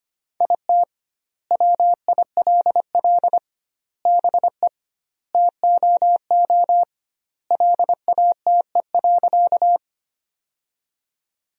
Morse code, heard as a keyed tone, about 25 wpm, 700 Hz. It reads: IT WILL BE TOO LATE.
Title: 2x Word-Spacing - Sentences from Top 300 Words - 25wpm